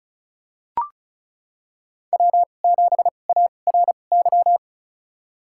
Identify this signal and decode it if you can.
Morse code, heard as a keyed tone, about 36 wpm, spaced as for 25 wpm, 700 Hz. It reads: W7ARY
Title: USA Call Signs - 5 Characters - 2x Character Spacing - 35wpm